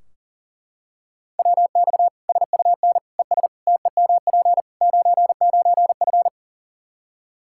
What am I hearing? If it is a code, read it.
Morse code, heard as a keyed tone, about 37 wpm, 700 Hz. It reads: WX SUN ES TEMP 99F
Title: Practice QSO - Exchange 2 - Info - 40wpm